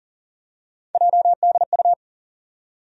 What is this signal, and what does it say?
Morse code, heard as a keyed tone, about 40 wpm, 700 Hz. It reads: JDU
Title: USA Call Sign Postfixes - 3 Letters 40wpm